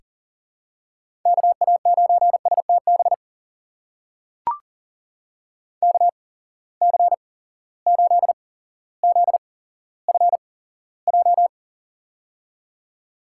Morse code, heard as a keyed tone, about 40 wpm, spaced as for 17 wpm, 700 Hz. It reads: KA9STB KC8ZFJ
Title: USA Call Signs - 6 Characters - 8x Character Spacing - 40wpm